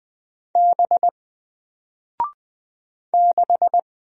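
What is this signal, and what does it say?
Morse code, heard as a keyed tone, about 20 wpm, 700 Hz. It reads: B 6
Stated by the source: Single letter and number Morse drill at 20 wpm, from N4LCD.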